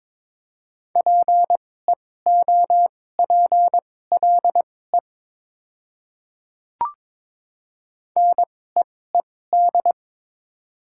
Morse code, heard as a keyed tone, about 21 wpm, spaced as for 16 wpm, 700 Hz. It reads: PEOPLE NEED
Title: 2x Character Spacing - Top 300 Words - 22wpm